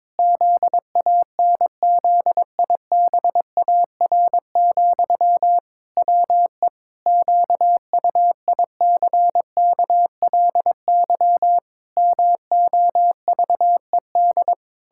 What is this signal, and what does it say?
Morse code, heard as a keyed tone, about 22 wpm, 700 Hz. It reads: ZANZIBAR, WE QUICKLY MOVED
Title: Pangram Challenge v2 - 22wpm